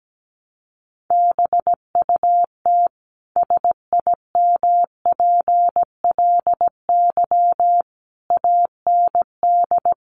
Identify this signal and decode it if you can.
Morse code, heard as a keyed tone, about 17 wpm, 700 Hz. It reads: BUT SIMPLY AND
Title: Sentences from Top 1300 Words 17wpm